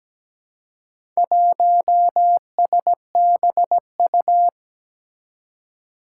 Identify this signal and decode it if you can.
Morse code, heard as a keyed tone, about 17 wpm, 700 Hz. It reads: 1SBU
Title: Four Challenging Letter-Number Combinations 17wpm